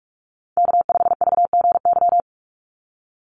Morse code, heard as a keyed tone, about 44 wpm, 700 Hz. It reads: K5VGY
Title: USA Call Signs 45wpm